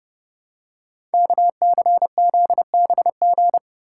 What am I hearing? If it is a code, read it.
Morse code, heard as a keyed tone, about 30 wpm, 700 Hz. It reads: KCZBG
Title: Code Groups - 5 Letters - 30wpm